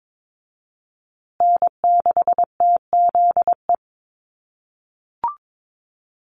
Morse code, heard as a keyed tone, about 22 wpm, 700 Hz. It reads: N6TZE
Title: USA Call Signs Repeated 2x 22wpm